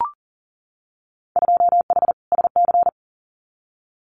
Morse code, heard as a keyed tone, about 40 wpm, 700 Hz. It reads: 2H SC
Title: Winter Field Day Exchanges 40wpm